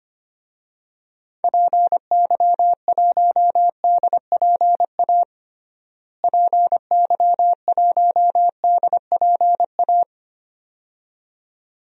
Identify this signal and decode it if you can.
Morse code, heard as a keyed tone, about 25 wpm, 700 Hz. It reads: PY1DPA PY1DPA
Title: DXCC-100 Call Signs - 6 Characters - Repeated 2x - 25wpm